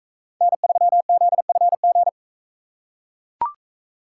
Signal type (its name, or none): Morse code